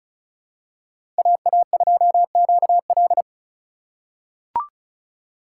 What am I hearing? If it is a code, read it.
Morse code, heard as a keyed tone, about 36 wpm, 700 Hz. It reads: AA2QL
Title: USA Call Signs - 5 Characters - 4x Character Spacing - 35wpm